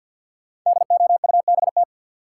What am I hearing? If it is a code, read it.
Morse code, heard as a keyed tone, about 53 wpm, 700 Hz. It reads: DOUBT
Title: Sets of 5 Words - New Word Drill - 50wpm